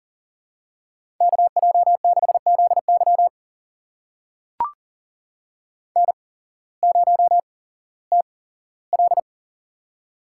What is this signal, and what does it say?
Morse code, heard as a keyed tone, about 40 wpm, spaced as for 19 wpm, 700 Hz. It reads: KJ6ZY N0TL
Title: USA Call Signs - 8x Character Spacing - 40wpm